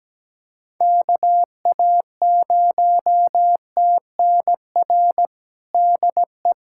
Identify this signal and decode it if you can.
Morse code, heard as a keyed tone, about 17 wpm, 700 Hz. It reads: KA0TNR DE
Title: USA Callsign DE Callsign 17wpm